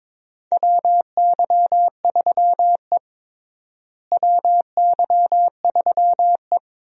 Morse code, heard as a keyed tone, about 22 wpm, 700 Hz. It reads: WY3E WY3E
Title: USA Call Signs - 4 Characters - Repeated 3x - 22wpm